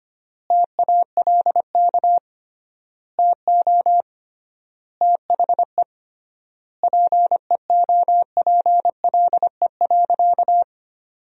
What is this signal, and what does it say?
Morse code, heard as a keyed tone, about 25 wpm, 700 Hz. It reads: TALK TO THE PEOPLE.
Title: 3x Word-Spacing - Sentences from Top 400 Words - 25wpm